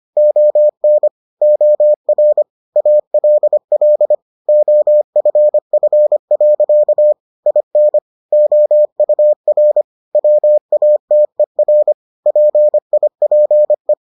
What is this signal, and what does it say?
Morse code, heard as a keyed tone, about 25 wpm, 600 Hz. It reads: ON OR ALL OFF. IN OUR WATER PIPE